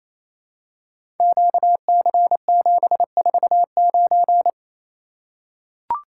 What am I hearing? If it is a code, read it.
Morse code, heard as a keyed tone, about 29 wpm, 700 Hz. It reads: QC749
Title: Code Groups - 5 Letter:Numbers - 28wpm